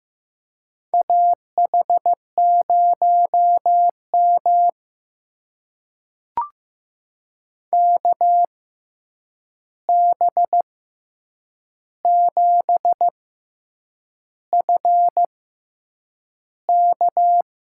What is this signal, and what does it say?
Morse code, heard as a keyed tone, about 15 wpm, spaced as for 7 wpm, 700 Hz. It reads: AH0M KB7FK
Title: USA Call Signs - 6x Character Spacing - 15wpm